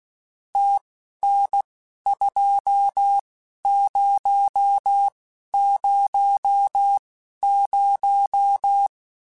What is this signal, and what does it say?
Morse code, heard as a keyed tone, about 16 wpm, spaced as for 11 wpm, 800 Hz. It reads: TN2000
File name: QSOs / QSOs-mp3s / WD0FBT 1@10